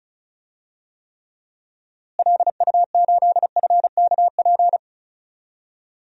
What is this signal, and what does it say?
Morse code, heard as a keyed tone, about 36 wpm, 700 Hz. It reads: LU8FKP
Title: DXCC-100 Call Signs - 6 Characters - Repeated 3x - 35wpm